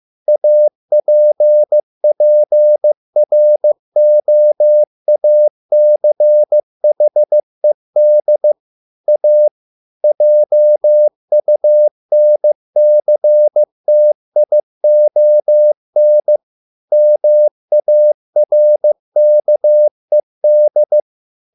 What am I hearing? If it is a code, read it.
Morse code, heard as a keyed tone, about 15 wpm, 600 Hz. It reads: APPROACHED A JUNCTION MARKED